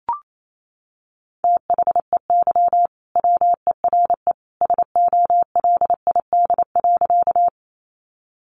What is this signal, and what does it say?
Morse code, heard as a keyed tone, about 28 wpm, 700 Hz. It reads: THEY WERE SOLID.
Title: Sentences from Top 1500 Words 28wpm